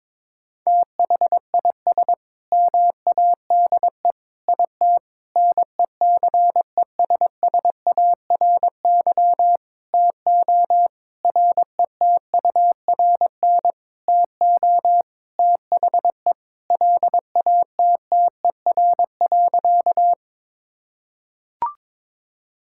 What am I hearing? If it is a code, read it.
Morse code, heard as a keyed tone, about 22 wpm, 700 Hz. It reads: THIS MADE IT NECESSARY TO RETURN TO THE LATTER.
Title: Sentences from Top 1600 Words 22wpm